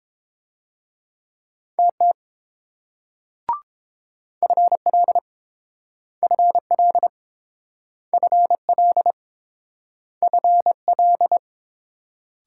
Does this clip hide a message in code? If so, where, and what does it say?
Yes: Morse code from 1.8 to 11.4 s, heard as a keyed tone, about 32 wpm, 700 Hz. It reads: TT FL FL FL FL